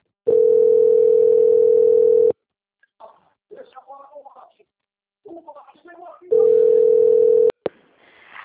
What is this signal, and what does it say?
Ringback tone (North America), 440 + 480 Hz, with speech.
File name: out- - - - - .25628